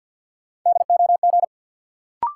Morse code, heard as a keyed tone, about 53 wpm, 700 Hz. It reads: DOG
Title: Sentences from Top 400 Words - New Word Drill - 50wpm